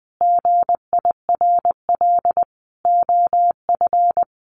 Morse code, heard as a keyed tone, about 20 wpm, 700 Hz. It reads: GIRL OF
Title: Sentences from Top 1400 Words 20wpm